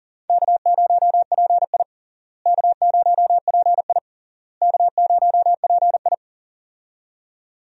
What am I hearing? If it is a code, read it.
Morse code, heard as a keyed tone, about 40 wpm, 700 Hz. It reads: K0PI K0PI K0PI
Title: USA Call Signs - 4 Characters - Repeated 3x - 40wpm